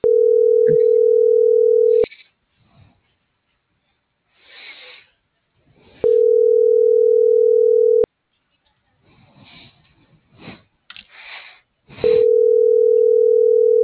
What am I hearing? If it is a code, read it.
Ringback tone (North America), 440 + 480 Hz.